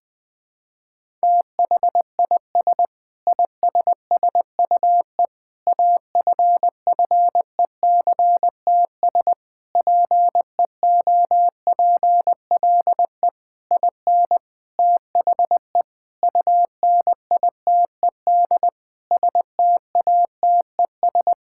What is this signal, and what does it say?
Morse code, heard as a keyed tone, about 21 wpm, 700 Hz. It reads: THIS ISSUE AFFECTS PEOPLE IN THE UNITED STATES
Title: Sentences with 8-Word Phrases - 20wpm